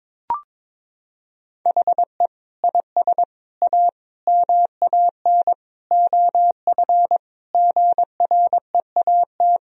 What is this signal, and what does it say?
Morse code, heard as a keyed tone, about 22 wpm, 700 Hz. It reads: HE IS A MAN OF GREAT
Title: Sentences from Top 1000 Words - Review Part 3 - 22wpm